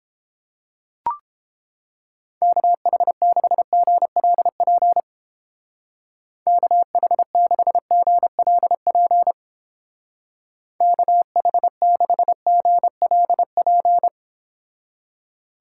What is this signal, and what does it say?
Morse code, heard as a keyed tone, about 31 wpm, 700 Hz. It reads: KH6GLP KH6GLP KH6GLP